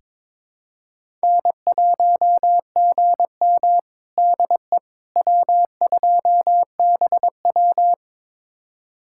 Morse code, heard as a keyed tone, about 22 wpm, 700 Hz. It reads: N1GM DE W2BW
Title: USA Callsign DE Callsign - 4 Characters - 22wpm